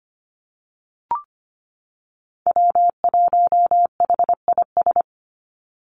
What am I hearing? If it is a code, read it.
Morse code, heard as a keyed tone, about 25 wpm, 700 Hz. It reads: W1HIS
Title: Call Signs - 25wpm